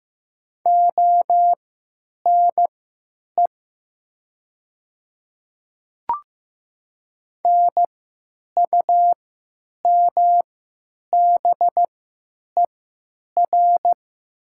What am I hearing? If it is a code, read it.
Morse code, heard as a keyed tone, about 15 wpm, spaced as for 9 wpm, 700 Hz. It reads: ONE NUMBER
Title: 3x Character Spacing - Top 300 Words - 15wpm